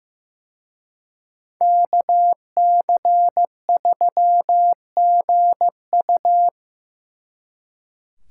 Morse code, heard as a keyed tone, about 18 wpm, 700 Hz. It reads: KC3GU
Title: Speed Racing - USA 2x2 Call Signs 15wpm